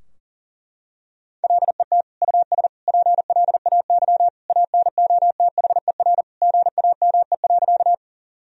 Morse code, heard as a keyed tone, about 40 wpm, 700 Hz. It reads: LET US PLAY ANOTHER GAME.
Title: Sentences from Top 1000 Words - Review Part 3 - 40wpm